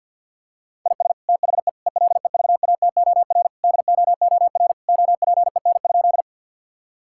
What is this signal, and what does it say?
Morse code, heard as a keyed tone, about 50 wpm, 700 Hz. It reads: IS THE ELEVATOR DOOR OPEN?
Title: Sentences from Top 1600 Words 50wpm